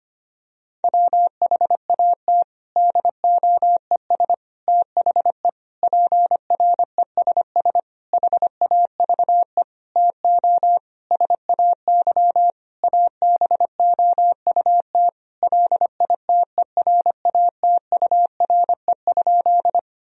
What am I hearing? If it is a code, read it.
Morse code, heard as a keyed tone, about 25 wpm, 700 Hz. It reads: WHAT DOES THE PRESS HAVE TO SAY ABOUT LITERATURE?
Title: Sentences from Top 1000 Words - Review Part 3 - 25wpm